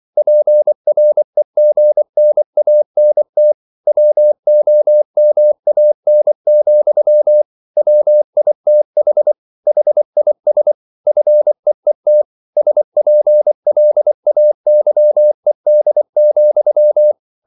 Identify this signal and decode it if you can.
Morse code, heard as a keyed tone, about 24 wpm, 600 Hz. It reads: PREGNANT WOMAN, WITH HIS FEET SPLAYED,